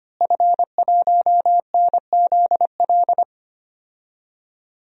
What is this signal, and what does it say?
Morse code, heard as a keyed tone, about 25 wpm, 700 Hz. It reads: F1NZL
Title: DXCC-100 Call Signs - 5 Characters - Repeated 3x - 25wpm